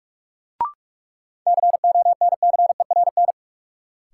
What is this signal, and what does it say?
Morse code, heard as a keyed tone, about 46 wpm, 700 Hz. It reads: CONCERN
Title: Sentences from Top 1000 Words - New Word Drill - 45wpm